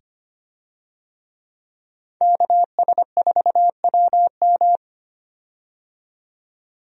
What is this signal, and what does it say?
Morse code, heard as a keyed tone, about 25 wpm, 700 Hz. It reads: KS4WM
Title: USA Call Signs - 5 Characters - 25wpm